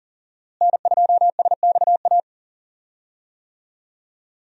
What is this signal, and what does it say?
Morse code, heard as a keyed tone, about 40 wpm, 700 Hz. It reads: N2SXA
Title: USA Call Signs - 5 Characters - Repeated 2x - 40wpm